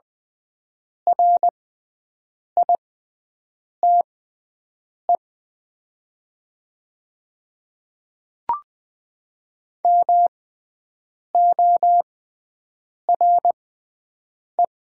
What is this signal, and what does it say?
Morse code, heard as a keyed tone, about 20 wpm, spaced as for 7 wpm, 690 Hz. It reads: RITE MORE
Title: 6x Character Spacing - Top 100 Words - 20wpm